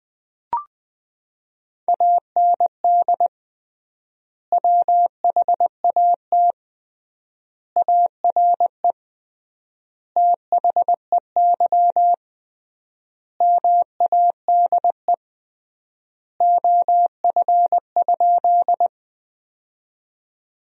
Morse code, heard as a keyed tone, about 20 wpm, 700 Hz. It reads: AND WHAT ARE THEY MADE OF?
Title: 3x Word-Spacing - Sentences from Top 200 Words - 20wpm